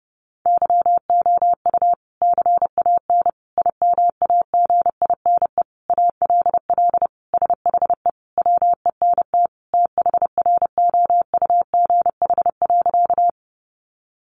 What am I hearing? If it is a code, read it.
Morse code, heard as a keyed tone, about 30 wpm, 700 Hz. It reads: YOU CAN IMAGINE ALL SHE WENT THROUGH.
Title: Sentences from Top 1400 Words 30wpm